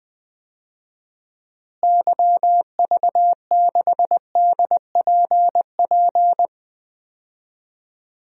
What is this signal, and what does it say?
Morse code, heard as a keyed tone, about 20 wpm, 700 Hz. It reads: YV6DPP
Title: DXCC-100 Call Signs - 6 Characters - Repeated 2x - 20wpm